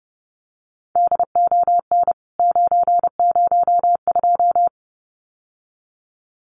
Morse code, heard as a keyed tone, about 30 wpm, 700 Hz. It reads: DON 902
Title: CWT Exchange 30wpm